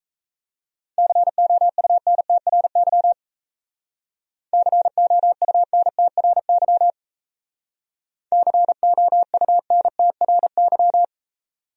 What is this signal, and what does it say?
Morse code, heard as a keyed tone, about 34 wpm, 700 Hz. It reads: COUNTRY COUNTRY COUNTRY
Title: Speed-Racing Top 300 Words 28wpm